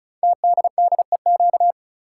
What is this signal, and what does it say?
Morse code, heard as a keyed tone, about 36 wpm, 700 Hz. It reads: TDDEQ
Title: Code Groups - 5 Letters - 35wpm